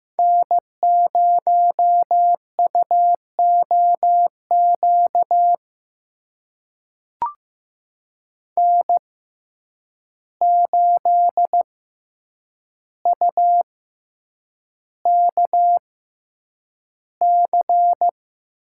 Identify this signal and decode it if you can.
Morse code, heard as a keyed tone, about 15 wpm, spaced as for 8 wpm, 700 Hz. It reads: N0UOQ N8UKC
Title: USA Call Signs - 5 Characters - 6x Character Spacing - 15wpm